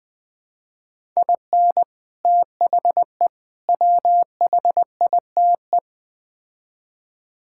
Morse code, heard as a keyed tone, about 20 wpm, 700 Hz. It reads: IN THE WHITE
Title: Build A Phrase - 4-Word Phrases - 20wpm